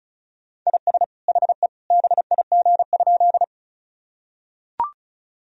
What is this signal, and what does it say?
Morse code, heard as a keyed tone, about 34 wpm, 700 Hz. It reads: IS HE BIG?
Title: Sentences from Top 1000 Words - Review Part 1 - 35wpm